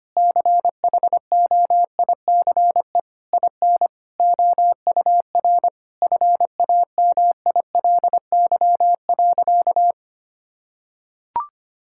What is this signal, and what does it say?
Morse code, heard as a keyed tone, about 25 wpm, 700 Hz. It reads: CHOICE IN OUR FAMILY.
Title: Sentences from Top 1000 Words - Review Part 6 - 25wpm